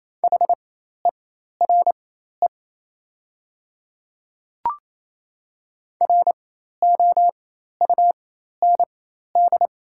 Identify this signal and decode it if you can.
Morse code, heard as a keyed tone, about 28 wpm, spaced as for 13 wpm, 700 Hz. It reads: HERE ROUND
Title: 4x Character Spacing - Top 300 Words - 28wpm